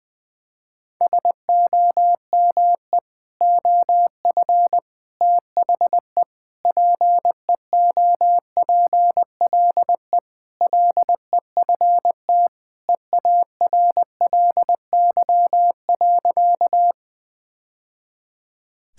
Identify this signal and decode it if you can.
Morse code, heard as a keyed tone, about 20 wpm, 700 Hz. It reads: SOME OF THE PEOPLE LEFT EARLY.
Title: Sentences with 3-Word Phrases - 20wpm